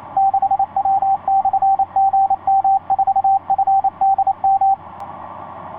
Morse code, heard as a keyed tone, about 28 wpm, 780 Hz, with radio noise. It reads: 6W/GM4FDM
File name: 6w-gm4fdm f6bee etc 15m cw 30jan2010 1502